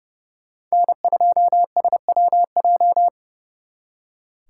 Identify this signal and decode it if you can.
Morse code, heard as a keyed tone, about 38 wpm, 700 Hz. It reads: N2SWJ